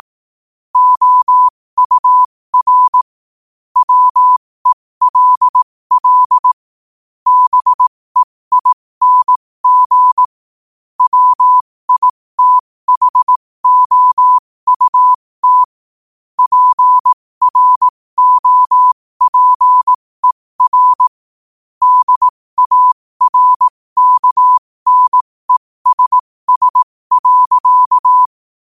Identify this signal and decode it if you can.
Morse code, heard as a keyed tone, about 18 wpm, spaced as for 15 wpm, 1000 Hz. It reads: OUR WELL BEING WITHOUT PROPER DARKNESS.
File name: qotdfiles / 15wpm